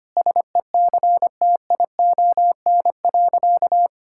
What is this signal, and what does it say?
Morse code, heard as a keyed tone, about 25 wpm, 700 Hz. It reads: SECTION.
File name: practice / 25wpm / Sentences with 9-Word Phrases - 25wpm